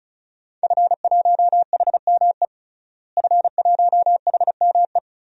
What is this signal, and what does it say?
Morse code, heard as a keyed tone, about 35 wpm, 700 Hz. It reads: F1HME F1HME
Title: DXCC-100 Call Signs - 5 Characters - Repeated 3x - 35wpm